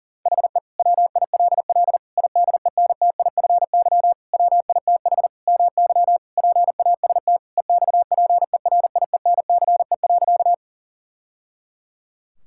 Morse code, heard as a keyed tone, about 40 wpm, 700 Hz. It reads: HE WILL IDENTIFY WITH MY PAST EXPERIENCE.
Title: Sentences from Top 1000 Words - Review Part 6 - 40wpm